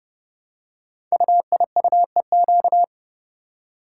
Morse code, heard as a keyed tone, about 30 wpm, 700 Hz. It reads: UIUEQ
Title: Code Groups - 5 Letters - 30wpm